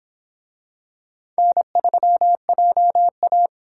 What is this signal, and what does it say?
Morse code, heard as a keyed tone, about 25 wpm, 700 Hz. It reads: N3JA